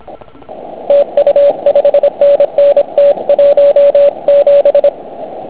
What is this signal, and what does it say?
Morse code, heard as a keyed tone, about 26 wpm, 570 Hz. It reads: TU 5NN T17
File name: Archiv / CUC149 / OK1IF CUC149